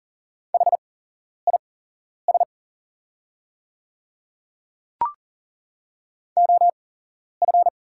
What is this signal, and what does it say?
Morse code, heard as a keyed tone, about 40 wpm, spaced as for 11 wpm, 700 Hz. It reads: HIS OF
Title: 8x Character Spacing - Top 200 Words - 40wpm